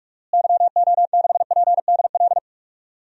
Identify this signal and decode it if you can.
Morse code, heard as a keyed tone, about 47 wpm, 700 Hz. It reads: YO6PDL
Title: DXCC-100 Call Signs - 6 Characters - Repeated 2x - 45wpm